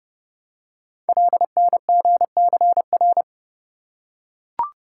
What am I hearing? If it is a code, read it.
Morse code, heard as a keyed tone, about 30 wpm, 700 Hz. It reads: LNGCR
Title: Code Groups - 5 Letters - 30wpm